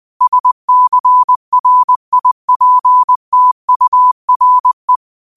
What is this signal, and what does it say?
Morse code, heard as a keyed tone, about 20 wpm, 1000 Hz. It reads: SCRIPTURE
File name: qotdfiles / 20wpm